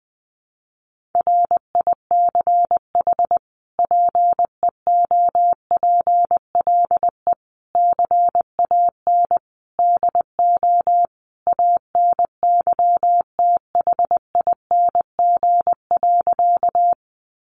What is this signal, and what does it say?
Morse code, heard as a keyed tone, about 20 wpm, 700 Hz. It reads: RICH PEOPLE CAN DO ANYTHING.
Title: Sentences from Top 1100 Words 20wpm